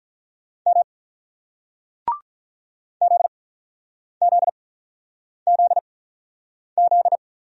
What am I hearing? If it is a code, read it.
Morse code, heard as a keyed tone, about 46 wpm, spaced as for 9 wpm, 700 Hz. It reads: M ZZZZ